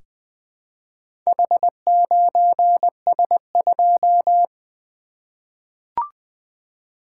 Morse code, heard as a keyed tone, about 20 wpm, 700 Hz. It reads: H9S2
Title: Four Letter-Number Combinations - 20wpm